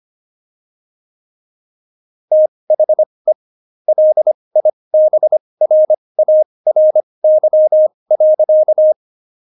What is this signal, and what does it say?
Morse code, heard as a keyed tone, about 25 wpm, spaced as for 20 wpm, 620 Hz. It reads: THE LIBRARY.
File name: Jokes-25-char-20-eff